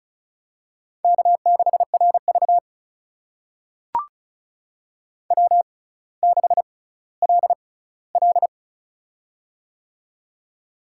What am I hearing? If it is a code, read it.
Morse code, heard as a keyed tone, about 36 wpm, spaced as for 18 wpm, 700 Hz. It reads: K6RV W6LL
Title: USA Call Signs - 4 Characters - 6x Character Spacing - 35wpm